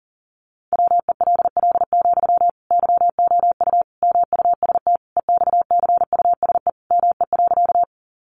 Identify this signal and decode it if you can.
Morse code, heard as a keyed tone, about 40 wpm, 700 Hz. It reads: WELL, YOU MUST EXCUSE ME.
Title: Sentences from Top 700 Words 40wpm